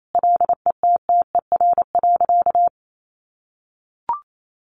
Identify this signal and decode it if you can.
Morse code, heard as a keyed tone, about 28 wpm, 700 Hz. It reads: LETTER.
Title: Sentences from Top 1400 Words 28wpm